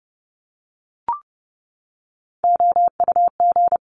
Morse code, heard as a keyed tone, about 30 wpm, 700 Hz. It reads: OUG
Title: Top Three Letter Combinations in Words 30wpm